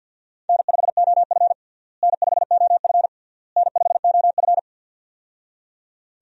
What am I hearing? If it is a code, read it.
Morse code, heard as a keyed tone, about 50 wpm, 700 Hz. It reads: N5OF N5OF N5OF
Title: USA Call Signs - 4 Characters - Repeated 3x - 50wpm